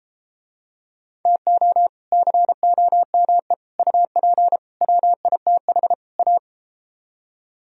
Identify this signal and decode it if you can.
Morse code, heard as a keyed tone, about 33 wpm, 700 Hz. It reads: TO COME UP WITH A